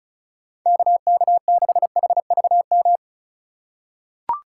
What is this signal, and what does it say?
Morse code, heard as a keyed tone, about 34 wpm, 700 Hz. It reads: KK6HVM
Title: USA Call Signs - 6 Characters - 2x Character Spacing - 35wpm